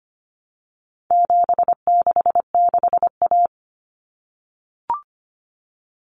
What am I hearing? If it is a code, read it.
Morse code, heard as a keyed tone, about 25 wpm, 700 Hz. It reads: 766A
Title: Common Cut Numbers - 4-Digit 25wpm